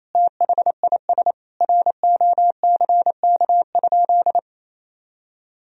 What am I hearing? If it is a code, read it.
Morse code, heard as a keyed tone, about 27 wpm, 700 Hz. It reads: THIS ROCK?
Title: Sentences from Top 1000 Words - Review Part 2 - 28wpm